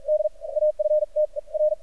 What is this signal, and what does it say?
Morse code, heard as a keyed tone, about 47 wpm, spaced as for 40 wpm, 610 Hz. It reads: D AFT R
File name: archived / CRSnet